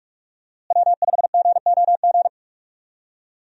Morse code, heard as a keyed tone, about 44 wpm, 700 Hz. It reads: W5GOG
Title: USA Call Signs - 5 Characters - Repeated 2x - 45wpm